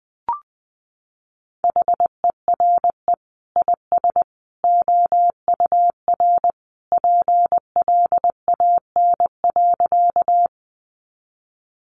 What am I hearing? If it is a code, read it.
Morse code, heard as a keyed tone, about 19 wpm, 700 Hz. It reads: HERE IS OUR PLAN.
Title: Sentences from Top 500 Words 20wpm